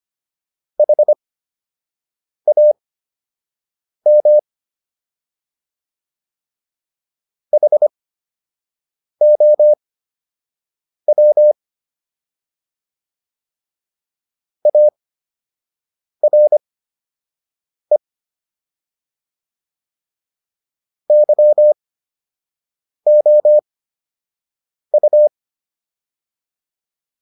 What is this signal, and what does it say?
Morse code, heard as a keyed tone, about 25 wpm, spaced as for 6 wpm, 600 Hz. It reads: HAM HOW ARE YOU